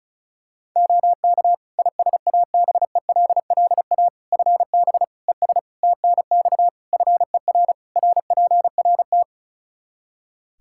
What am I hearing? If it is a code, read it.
Morse code, heard as a keyed tone, about 36 wpm, 700 Hz. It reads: OK ISABELLA FB ES TNX FER RPRT
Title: Practice QSO - Exchange 2 - Info - 35wpm